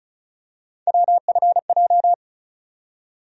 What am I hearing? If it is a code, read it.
Morse code, heard as a keyed tone, about 36 wpm, 700 Hz. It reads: WFJ
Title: USA Call Sign Postfixes - 3 Letters 35wpm